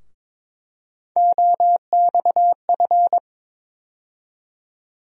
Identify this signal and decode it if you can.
Morse code, heard as a keyed tone, about 22 wpm, 700 Hz. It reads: OXF
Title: USA Call Sign Postfixes - 3 Letters 22wpm